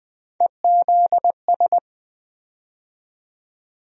Morse code, heard as a keyed tone, about 20 wpm, 700 Hz. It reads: EZS
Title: USA Call Sign Postfixes - 3 Letters 20wpm